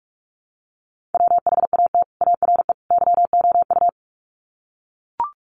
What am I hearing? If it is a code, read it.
Morse code, heard as a keyed tone, about 44 wpm, 700 Hz. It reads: WHAT ARE YOU
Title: Sets of 3 Words 45wpm